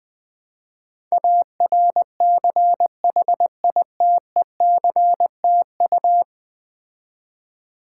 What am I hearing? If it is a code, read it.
Morse code, heard as a keyed tone, about 20 wpm, 700 Hz. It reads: ARCHITECTU
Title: Build A Word - 5 Letters At A Time - 12-Letter Words - 20wpm